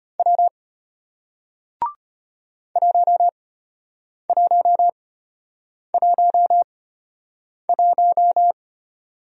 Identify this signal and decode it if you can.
Morse code, heard as a keyed tone, about 34 wpm, spaced as for 8 wpm, 700 Hz. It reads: W 1111